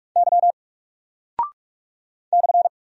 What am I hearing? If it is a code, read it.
Morse code, heard as a keyed tone, about 44 wpm, 700 Hz. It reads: Y /
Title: Single Character (Letter:Number:Punctuation:Prosign) 45wpm